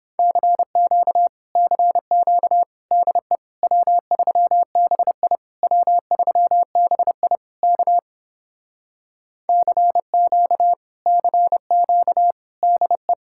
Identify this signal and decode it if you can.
Morse code, heard as a keyed tone, about 29 wpm, 700 Hz. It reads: CQ CQ DE W3BI W3BI K CQ CQ DE